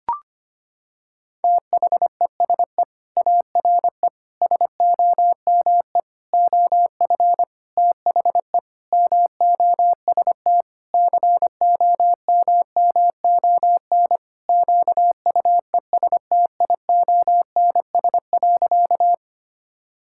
Morse code, heard as a keyed tone, about 25 wpm, 700 Hz. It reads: THESE ARE SOME OF THE MOST COMMON QUESTIONS.
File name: practice / 25wpm / Sentences with 4-Word Phrases - 25wpm